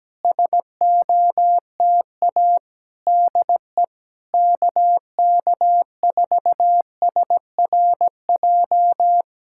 Morse code, heard as a keyed tone, about 17 wpm, 700 Hz. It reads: SOTA DE KK4SRJ
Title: Speed-Racing Common QSO Phrases - 17wpm